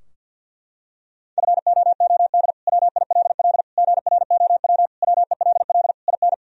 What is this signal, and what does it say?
Morse code, heard as a keyed tone, about 53 wpm, 700 Hz. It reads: FOOD WILL GROW WELL IN
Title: Sentences from Top 1000 Words - Review Part 2 - 50wpm